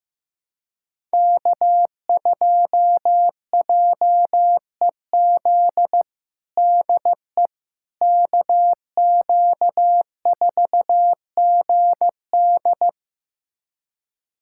Morse code, heard as a keyed tone, about 15 wpm, 700 Hz. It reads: K2JEZ DE KQ4GD
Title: USA Callsign DE Callsign - 5 Characters - 15wpm